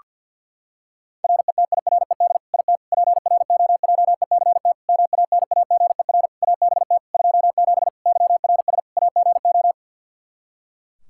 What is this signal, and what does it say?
Morse code, heard as a keyed tone, about 50 wpm, 700 Hz. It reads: RETIRED IT PROJECT MANAGER ABT 27 YRS AGO
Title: Practice QSO - Exchange 3 - More Info - 50wpm